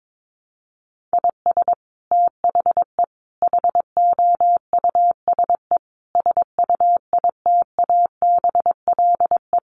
Morse code, heard as a keyed tone, about 22 wpm, 700 Hz. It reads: IS THE HOUSE SUITABLE
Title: Sentences from Top 1400 Words 22wpm